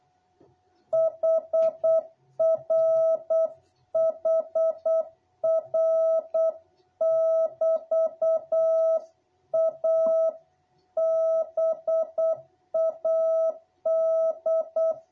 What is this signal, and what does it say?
Morse code, heard as a keyed tone, about 8 wpm, 660 Hz. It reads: HR HR<BT>A BAD
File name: jh8cbh / 2021cw